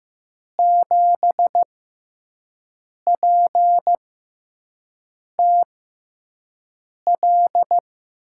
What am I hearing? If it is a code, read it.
Morse code, heard as a keyed tone, about 15 wpm, 700 Hz. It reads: 7 P T L
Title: USA Call Signs - 6x Character Spacing - 15wpm